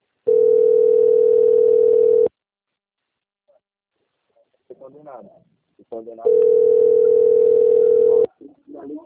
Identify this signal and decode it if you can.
Ringback tone (North America), 440 + 480 Hz, with speech.
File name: out- - - - - .90073